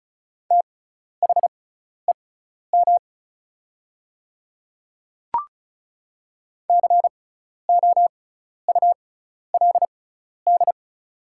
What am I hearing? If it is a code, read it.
Morse code, heard as a keyed tone, about 34 wpm, spaced as for 12 wpm, 700 Hz. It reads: THEM COULD
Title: 6x Character Spacing - Top 100 Words - 35wpm